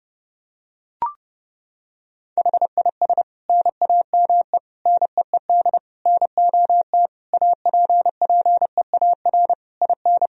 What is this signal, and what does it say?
Morse code, heard as a keyed tone, about 30 wpm, 700 Hz. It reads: HIS NAME NEED NOT APPEAR IN
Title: Sentences from Top 1000 Words - Review Part 2 - 30wpm